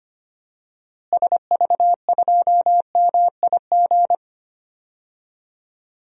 Morse code, heard as a keyed tone, about 25 wpm, 700 Hz. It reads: SV2MIG
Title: DXCC-100 Call Signs - 6 Characters - Repeated 3x - 25wpm